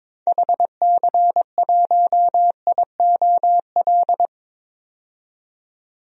Morse code, heard as a keyed tone, about 22 wpm, 700 Hz. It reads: HC1IOL
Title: DXCC-100 Call Signs - 6 Characters - 22wpm